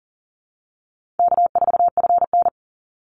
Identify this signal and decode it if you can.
Morse code, heard as a keyed tone, about 40 wpm, 700 Hz. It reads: K4FN
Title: Easy Call Signs Repeated 3x - 40wpm